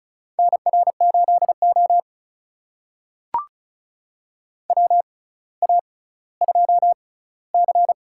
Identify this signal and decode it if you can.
Morse code, heard as a keyed tone, about 36 wpm, spaced as for 18 wpm, 700 Hz. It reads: NR8O WA2C
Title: USA Call Signs - 4 Characters - 6x Character Spacing - 35wpm